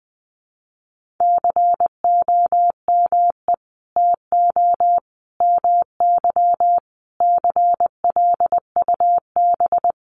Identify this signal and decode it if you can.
Morse code, heard as a keyed tone, about 20 wpm, 700 Hz. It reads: COME TO MY CLUB
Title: Sentences from Top 1500 Words 20wpm